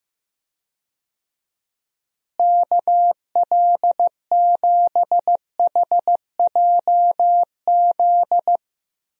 Morse code, heard as a keyed tone, about 16 wpm, 700 Hz. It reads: KL7HJZ
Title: DXCC-100 Call Signs - 6 Characters - Repeated 2x - 15wpm